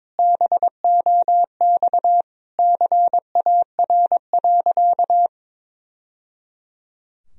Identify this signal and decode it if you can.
Morse code, heard as a keyed tone, about 22 wpm, 700 Hz. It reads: BOX CAR.
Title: Sentences from Top 1000 Words - Review Part 2 - 22wpm